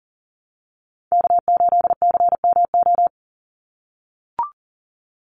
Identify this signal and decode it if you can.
Morse code, heard as a keyed tone, about 40 wpm, 700 Hz. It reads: K8CMO
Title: Call Signs - 40wpm